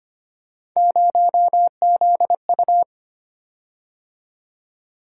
Morse code, heard as a keyed tone, about 25 wpm, 700 Hz. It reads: 0ZU
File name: Three Letter-Number Combinations - 25wpm